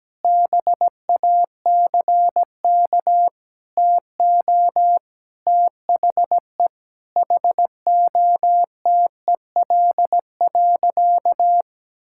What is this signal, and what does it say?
Morse code, heard as a keyed tone, about 17 wpm, 700 Hz. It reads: BACK TO THE HOTEL.
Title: Sentences from Top 1600 Words 17wpm